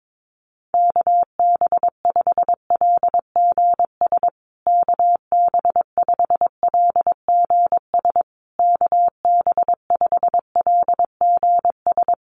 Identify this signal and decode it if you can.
Morse code, heard as a keyed tone, about 22 wpm, 700 Hz. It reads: KB5LGS KB5LGS KB5LGS
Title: USA Call Signs Repeated 3x 22wpm